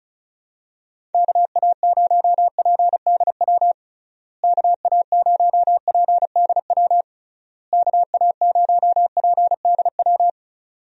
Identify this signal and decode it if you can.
Morse code, heard as a keyed tone, about 36 wpm, 700 Hz. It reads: KA0PDW KA0PDW KA0PDW
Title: USA Call Signs - 6 Characters - Repeated 3x - 35wpm